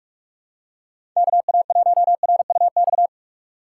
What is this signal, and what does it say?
Morse code, heard as a keyed tone, about 42 wpm, 700 Hz. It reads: KA1RUX
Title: USA Call Signs - 6 Characters - 3x Character Spacing - 45wpm